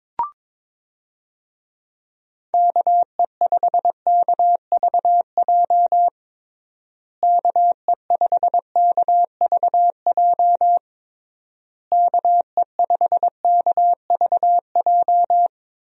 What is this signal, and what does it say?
Morse code, heard as a keyed tone, about 22 wpm, 700 Hz. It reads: KE5KVJ KE5KVJ KE5KVJ
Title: USA Call Signs - 6 Characters - Repeated 3x - 22wpm